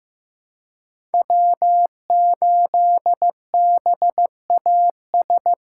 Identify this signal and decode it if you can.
Morse code, heard as a keyed tone, about 15 wpm, 700 Hz. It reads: W8BAS
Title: USA Call Signs - 5 Characters - 2x Character Spacing - 15wpm